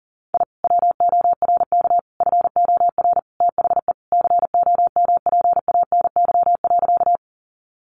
Morse code, heard as a keyed tone, about 37 wpm, 700 Hz. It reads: I WORK FOR THE COMPANY.
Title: Sentences from Top 600 Words 40wpm